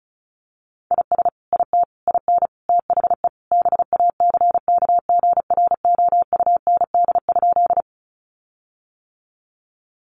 Morse code, heard as a keyed tone, about 34 wpm, 700 Hz. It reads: IS IT IN THE BACKGROUND?
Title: Sentences from Top 1300 Words 35wpm